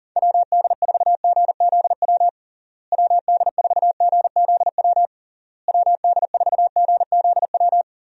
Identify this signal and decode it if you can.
Morse code, heard as a keyed tone, about 37 wpm, 700 Hz. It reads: WD4GZW WD4GZW WD4GZW
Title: USA Call Signs - 6 Characters - Repeated 3x - 40wpm